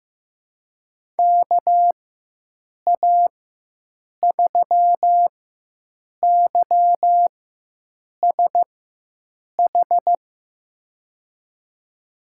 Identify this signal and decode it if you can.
Morse code, heard as a keyed tone, about 15 wpm, spaced as for 7 wpm, 700 Hz. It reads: KA3YSH
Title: USA Call Signs - 6 Characters - 4x Character Spacing - 15wpm